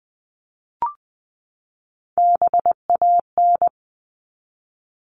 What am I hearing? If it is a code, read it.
Morse code, heard as a keyed tone, about 20 wpm, 700 Hz. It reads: BAN
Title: 260 Lesson 42e - Words 20-20wpm